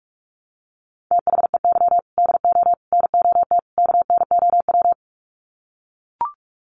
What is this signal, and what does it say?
Morse code, heard as a keyed tone, about 44 wpm, 700 Hz. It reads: THEY DO NOT KNOW
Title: Sets of 4 Words 45wpm